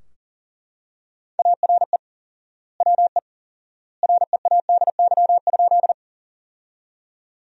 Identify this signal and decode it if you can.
Morse code, heard as a keyed tone, about 40 wpm, 700 Hz. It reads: ARE WE READY?
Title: 4x Word-Spacing - Sentences from Top 400 Words - 40wpm